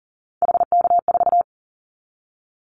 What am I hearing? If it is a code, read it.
Morse code, heard as a keyed tone, about 40 wpm, 690 Hz. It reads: HK4
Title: Callsign Prefixes for Top 100 DXCC - 40wpm